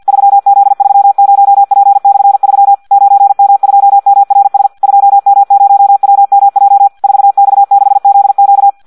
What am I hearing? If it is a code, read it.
Morse code, heard as a keyed tone, about 50 wpm, 800 Hz. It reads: 3Z30P/3 9M2MRS 2M0WMJ 4X6ZQ
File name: Audio practica CW / qrq-tx-50 2